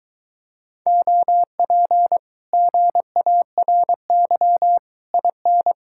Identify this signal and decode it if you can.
Morse code, heard as a keyed tone, about 22 wpm, 700 Hz. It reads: OP GARY IN